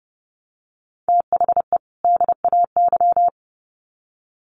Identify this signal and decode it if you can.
Morse code, heard as a keyed tone, about 29 wpm, 700 Hz. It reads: THE DAY
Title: Sets of 2 Words 30wpm